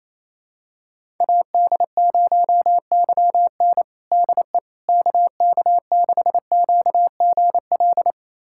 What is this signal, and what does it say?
Morse code, heard as a keyed tone, about 28 wpm, 700 Hz. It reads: AD0YN DE KK6QGL
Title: USA Callsign DE Callsign 28wpm